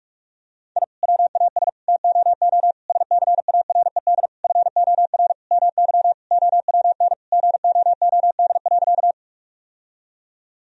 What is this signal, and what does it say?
Morse code, heard as a keyed tone, about 46 wpm, 700 Hz. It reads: I WAS TOO SCARED FOR MY OWN GOOD.
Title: Sentences from Top 1000 Words - Review Part 4 - 45wpm